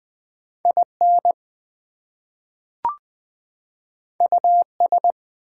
Morse code, heard as a keyed tone, about 20 wpm, 700 Hz. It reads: IN US
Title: Two Letter Words - 20wpm